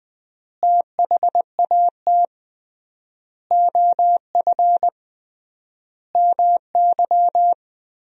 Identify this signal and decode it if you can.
Morse code, heard as a keyed tone, about 21 wpm, 700 Hz. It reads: THAT OF MY
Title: 3x Word-Spacing - Sentences from Top 500 Words - 20wpm